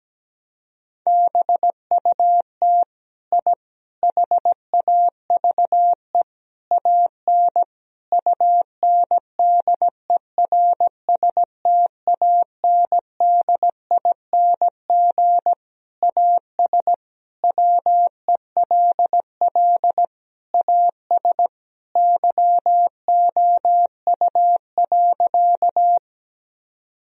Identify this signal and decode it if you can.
Morse code, heard as a keyed tone, about 17 wpm, 700 Hz. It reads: BUT I HAVE AN UNDERSTANDING AS WELL AS YOU.
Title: Sentences from Top 1000 Words - Review Part 3 - 17wpm